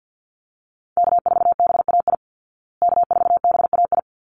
Morse code, heard as a keyed tone, about 48 wpm, 700 Hz. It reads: K4BAI K4BAI
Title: Call Signs Repeated 3x - 50wpm